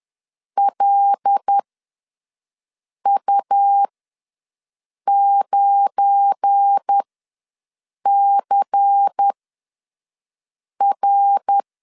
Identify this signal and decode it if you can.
Morse code, heard as a keyed tone, about 10 wpm, spaced as for 5 wpm, 800 Hz. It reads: LU9CR
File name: LEZIONI / 8 YJ45 / 8 YJ45 ws 10wpm 3843